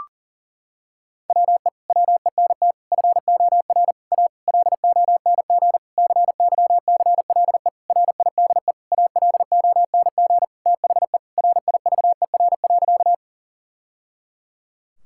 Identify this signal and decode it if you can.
Morse code, heard as a keyed tone, about 40 wpm, 700 Hz. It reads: WE WENT FOR A LONG CYCLE RIDE ALONG THE RIVER.
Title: Sentences from Top 1000 Words - Review Part 6 - 40wpm